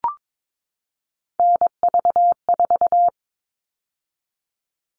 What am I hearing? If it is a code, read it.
Morse code, heard as a keyed tone, about 22 wpm, 700 Hz. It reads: NV4
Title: All Cut Numbers - 3-Digit 22wpm